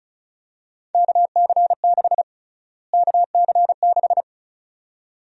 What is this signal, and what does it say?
Morse code, heard as a keyed tone, about 36 wpm, 700 Hz. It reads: KC6 KC6
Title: USA Call Sign Prefixes - Repeated 2x - 35wpm